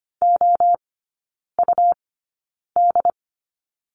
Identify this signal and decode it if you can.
Morse code, heard as a keyed tone, about 25 wpm, spaced as for 9 wpm, 700 Hz. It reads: OUD
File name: Lesson 17e - Letter u - Three Characters 25wpm